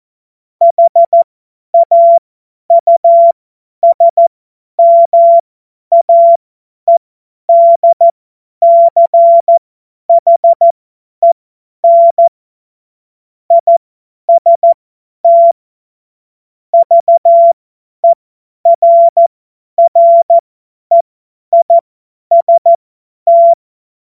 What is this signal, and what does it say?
Morse code, heard as a keyed tone, about 14 wpm, spaced as for 10 wpm, 680 Hz. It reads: HAUSMAEDCHEN IST VERREIST
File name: Morsekurs / Witze / Witz 22. Februar 2022 50 F70